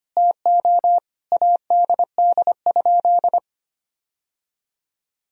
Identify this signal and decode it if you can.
Morse code, heard as a keyed tone, about 25 wpm, 700 Hz. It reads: TO ADD?
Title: Sentences from Top 1000 Words - Review Part 1 - 25wpm